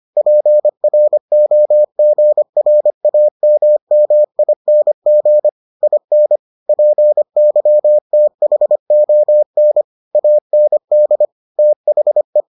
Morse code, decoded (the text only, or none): PROGRAMMING IN PYTHON AND THE